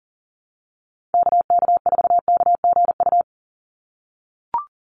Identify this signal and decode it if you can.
Morse code, heard as a keyed tone, about 37 wpm, 700 Hz. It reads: KK4KGU
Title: USA Call Signs 40wpm